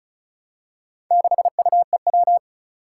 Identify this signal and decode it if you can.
Morse code, heard as a keyed tone, about 36 wpm, 690 Hz. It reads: 6UEW
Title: Four Challenging Letter-Number Combinations 35wpm